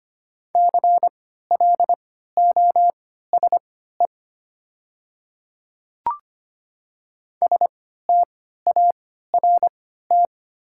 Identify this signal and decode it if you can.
Morse code, heard as a keyed tone, about 25 wpm, spaced as for 14 wpm, 700 Hz. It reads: CLOSE START
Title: 3x Character Spacing - Top 300 Words - 25wpm